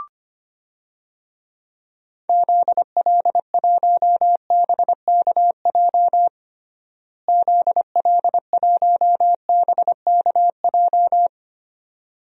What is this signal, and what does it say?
Morse code, heard as a keyed tone, about 25 wpm, 700 Hz. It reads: ZL1BKJ ZL1BKJ
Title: DXCC-100 Call Signs - 6 Characters - Repeated 2x - 25wpm